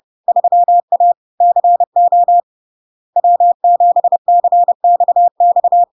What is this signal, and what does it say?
Morse code, heard as a keyed tone, about 30 wpm, 700 Hz. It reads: 3A CO W7CXX
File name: LARK-FD-2022-Log-in-CW-30wpm-700-Hz